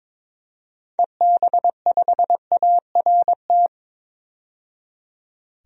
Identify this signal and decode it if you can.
Morse code, heard as a keyed tone, about 22 wpm, 700 Hz. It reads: EB5ART
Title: DXCC-100 Call Signs - 6 Characters - 22wpm